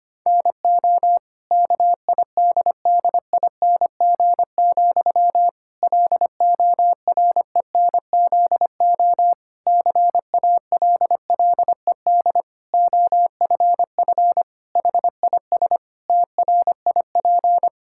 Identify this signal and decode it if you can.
Morse code, heard as a keyed tone, about 25 wpm, 700 Hz. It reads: NO KIDDING, LORENZO CALLED OFF HIS TRIP
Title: Pangram Challenge v2 - 25wpm